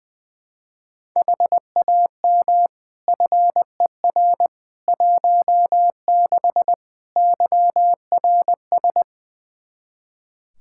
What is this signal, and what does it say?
Morse code, heard as a keyed tone, about 20 wpm, 700 Hz. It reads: HAM FER 16 YRS
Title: Practice QSO - Exchange 3 - More Info - 20wpm